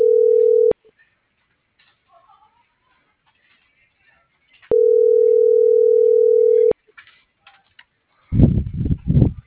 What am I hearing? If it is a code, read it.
Ringback tone (North America), 440 + 480 Hz.